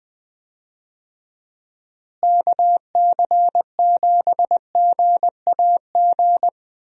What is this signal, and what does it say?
Morse code, heard as a keyed tone, about 20 wpm, 700 Hz. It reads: KC7GAG
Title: USA Call Signs - 6 Characters - Repeated 2x - 20wpm